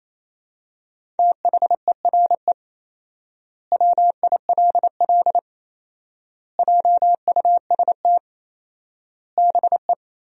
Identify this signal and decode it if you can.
Morse code, heard as a keyed tone, about 28 wpm, 700 Hz. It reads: THERE WILL JUST BE
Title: 4x Word-Spacing - Sentences from Top 200 Words - 28wpm